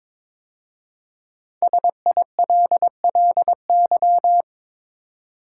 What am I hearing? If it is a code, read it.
Morse code, heard as a keyed tone, about 22 wpm, 700 Hz. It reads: SILLY
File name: 5 Letter Words - First 3 Letters Given - 22wpm